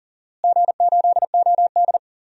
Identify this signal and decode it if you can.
Morse code, heard as a keyed tone, about 40 wpm, 700 Hz. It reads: G8OD
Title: DXCC-100 Call Signs - 4 Characters - Repeated 3x - 40wpm